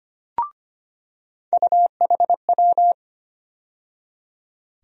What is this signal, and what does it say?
Morse code, heard as a keyed tone, about 25 wpm, 700 Hz. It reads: UHW
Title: USA Call Sign Postfixes - 3 Letters 25wpm